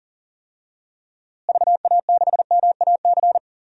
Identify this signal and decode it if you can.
Morse code, heard as a keyed tone, about 40 wpm, 700 Hz. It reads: VA6MAC
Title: DXCC-100 Call Signs - 6 Characters - Repeated 3x - 40wpm